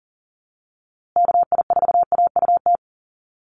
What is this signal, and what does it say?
Morse code, heard as a keyed tone, about 40 wpm, 700 Hz. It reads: KI4AUT
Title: USA Call Signs Repeated 2x 40wpm